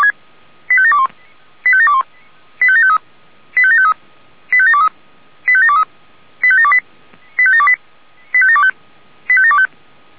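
Five-tone selective call (ZVEI), heard as quick runs of five tones, tones of about 70 ms. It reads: 85721 85721 85753 85753 85723 85723 85728 85728 85725
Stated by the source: ZVEI five-tone selective call received by SDR.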